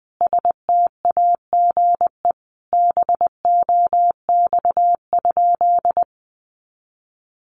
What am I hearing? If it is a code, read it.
Morse code, heard as a keyed tone, about 19 wpm, 700 Hz. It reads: STAGE BOX?
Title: Sentences from Top 1200 Words 20wpm